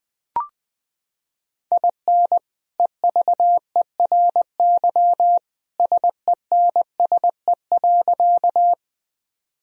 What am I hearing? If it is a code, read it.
Morse code, heard as a keyed tone, about 21 wpm, 700 Hz. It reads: IN EVERY SENSE.
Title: Sentences from Top 1000 Words - Review Part 4 - 20wpm